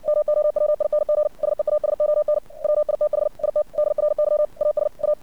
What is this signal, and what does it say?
Morse code, heard as a keyed tone, about 61 wpm, 600 Hz. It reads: MOWING SEASON WITH IT DRY AS A
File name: archived / CRSnet